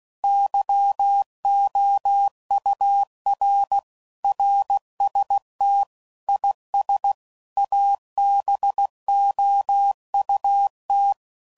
Morse code, heard as a keyed tone, about 16 wpm, 800 Hz. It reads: YOUR RST IS ABOUT
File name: CW Code Groups @15WPM/QSOs/WD0FBT Spaced